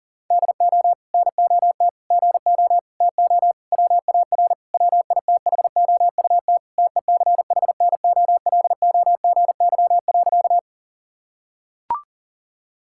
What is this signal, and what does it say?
Morse code, heard as a keyed tone, about 37 wpm, 700 Hz. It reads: DO NOT GO TO WAR WITHOUT TECHNOLOGY.
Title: Sentences from Top 1000 Words - Review Part 5 - 40wpm